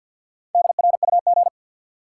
Morse code, heard as a keyed tone, about 54 wpm, 700 Hz. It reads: DRUG
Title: Sets of 5 Words - New Word Drill - 50wpm